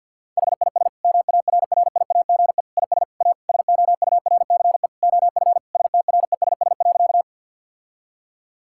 Morse code, heard as a keyed tone, about 50 wpm, 700 Hz. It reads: HIS MARRIAGE IS A SOURCE OF STRESS.
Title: Sentences from Top 1000 Words - Review Part 6 - 50wpm